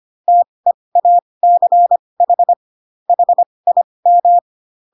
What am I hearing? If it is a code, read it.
Morse code, heard as a keyed tone, about 25 wpm, spaced as for 20 wpm, 700 Hz. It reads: TEACH HIM